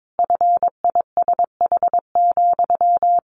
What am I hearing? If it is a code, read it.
Morse code, heard as a keyed tone, about 22 wpm, 700 Hz. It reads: FISH,
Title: Easy Phrases 22wpm